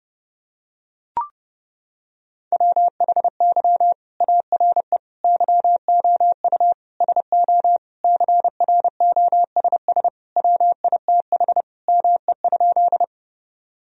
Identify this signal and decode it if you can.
Morse code, heard as a keyed tone, about 30 wpm, 700 Hz. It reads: WHY ARE YOU SO CROSS WITH ME?
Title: Sentences from Top 1000 Words - Review Part 1 - 30wpm